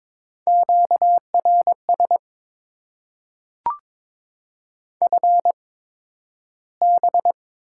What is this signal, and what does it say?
Morse code, heard as a keyed tone, about 22 wpm, spaced as for 12 wpm, 700 Hz. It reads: QRS FB
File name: Top 90 QSO Elements - 8x Character Spacing - 22wpm